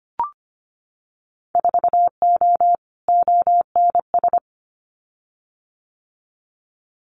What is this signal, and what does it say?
Morse code, heard as a keyed tone, about 25 wpm, 700 Hz. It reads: 4O ONS
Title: Winter Field Day Exchanges 25wpm